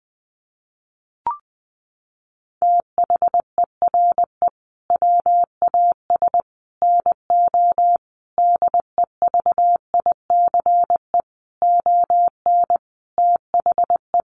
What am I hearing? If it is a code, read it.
Morse code, heard as a keyed tone, about 20 wpm, 700 Hz. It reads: THERE WAS NO DEVICE ON THE
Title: Sentences from Top 1500 Words 20wpm